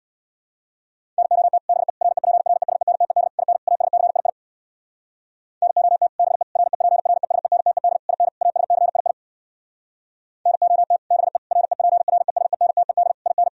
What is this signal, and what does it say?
Morse code, heard as a keyed tone, about 63 wpm, 700 Hz. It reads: NOT BE REPRESENTED IN ASCII NOT BE REPRESENTED IN ASCII NOT BE REPRESENTED IN